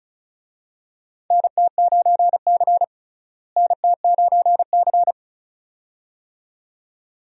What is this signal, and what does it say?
Morse code, heard as a keyed tone, about 34 wpm, 700 Hz. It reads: NT9C NT9C
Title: USA Call Signs - 4 Characters - Repeated 2x - 35wpm